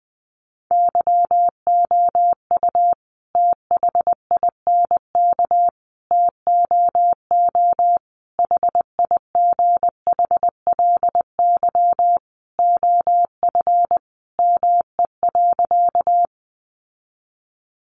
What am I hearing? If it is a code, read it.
Morse code, heard as a keyed tone, about 20 wpm, 700 Hz. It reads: YOU THINK TOO HIGHLY OF ME.
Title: Sentences from Top 1100 Words 20wpm